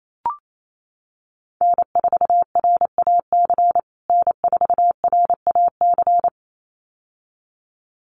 Morse code, heard as a keyed tone, about 28 wpm, 700 Hz. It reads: N4RAC N4RAC
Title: USA Call Signs Repeated 2x 28wpm